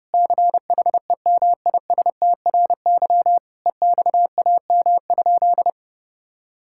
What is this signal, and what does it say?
Morse code, heard as a keyed tone, about 30 wpm, 700 Hz. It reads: CHEMISTRY EXAM?
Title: Sentences from Top 1000 Words - Review Part 6 - 30wpm